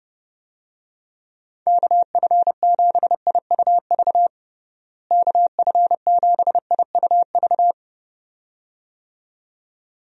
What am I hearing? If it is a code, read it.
Morse code, heard as a keyed tone, about 30 wpm, 700 Hz. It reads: KF7IUV KF7IUV
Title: USA Call Signs - 6 Characters - Repeated 2x - 30wpm